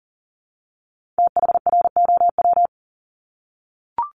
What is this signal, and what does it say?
Morse code, heard as a keyed tone, about 39 wpm, 700 Hz. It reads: THROW
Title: Multi-letter Phonemes 40wpm